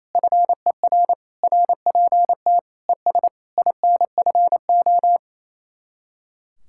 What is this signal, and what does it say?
Morse code, heard as a keyed tone, about 36 wpm, 700 Hz. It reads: FER RPT ES INFO